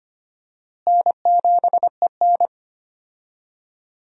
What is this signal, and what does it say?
Morse code, heard as a keyed tone, about 25 wpm, 700 Hz. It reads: N7EN
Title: USA Call Signs - 4 Characters - Repeated 3x - 25wpm